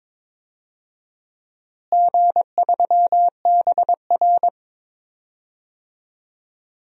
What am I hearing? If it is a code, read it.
Morse code, heard as a keyed tone, about 22 wpm, 700 Hz. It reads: G3BR
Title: DXCC-100 Call Signs - 4 Characters - Repeated 3x - 22wpm